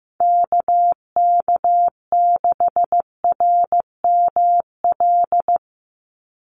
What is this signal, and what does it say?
Morse code, heard as a keyed tone, about 15 wpm, 700 Hz. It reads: KK6RML
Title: USA Call Signs 15wpm